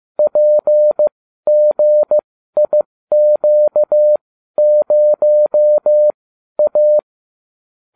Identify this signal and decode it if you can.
Morse code, heard as a keyed tone, about 15 wpm, spaced as for 12 wpm, 600 Hz. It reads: PGIQ0A